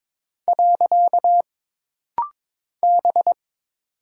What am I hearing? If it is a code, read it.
Morse code, heard as a keyed tone, about 22 wpm, 700 Hz. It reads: . B
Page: This rapid-fire timing is geared to push your ICR (Instant Character Recognition) skills.